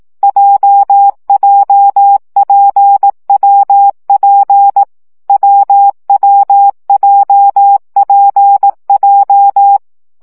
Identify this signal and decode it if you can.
Morse code, heard as a keyed tone, about 18 wpm, 800 Hz. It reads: JJPWP WWJPJ
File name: CW gyakorlas w p j 15 baud